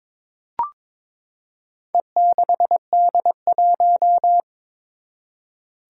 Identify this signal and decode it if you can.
Morse code, heard as a keyed tone, about 22 wpm, 700 Hz. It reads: E6D1
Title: Four Challenging Letter-Number Combinations 22wpm